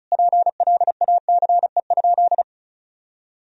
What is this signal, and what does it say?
Morse code, heard as a keyed tone, about 34 wpm, 700 Hz. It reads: PLACE?
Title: Sentences from Top 200 Words and Abbreviations 35wpm